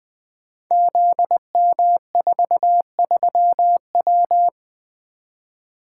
Morse code, heard as a keyed tone, about 20 wpm, 700 Hz. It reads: ZM43W
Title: Code Groups - 5 Letter:Numbers - 20wpm